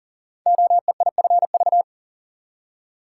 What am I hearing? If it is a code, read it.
Morse code, heard as a keyed tone, about 38 wpm, 690 Hz. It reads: OEIFV
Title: Code Groups - 5 Letters - 40wpm